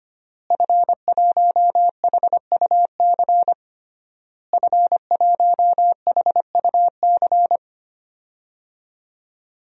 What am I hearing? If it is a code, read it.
Morse code, heard as a keyed tone, about 25 wpm, 700 Hz. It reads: F1HUC F1HUC
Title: DXCC-100 Call Signs - 5 Characters - Repeated 3x - 25wpm